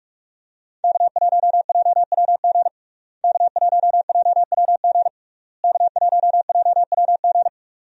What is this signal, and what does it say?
Morse code, heard as a keyed tone, about 42 wpm, 700 Hz. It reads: K1JWG K1JWG K1JWG
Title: USA Call Signs - 5 Characters - Repeated 3x - 45wpm